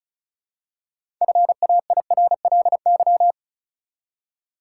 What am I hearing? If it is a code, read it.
Morse code, heard as a keyed tone, about 34 wpm, 700 Hz. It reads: FAIRLY
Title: Six Letter Words - 35wpm